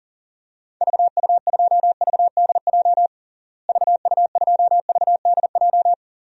Morse code, heard as a keyed tone, about 40 wpm, 700 Hz. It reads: VU2VDJ VU2VDJ
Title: DXCC-100 Call Signs - 6 Characters - Repeated 2x - 40wpm